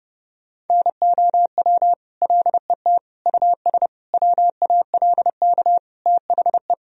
Morse code, heard as a keyed tone, about 30 wpm, 700 Hz. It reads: NOW LET US WALK THE
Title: Sentences from Top 1000 Words - Review Part 1 - 30wpm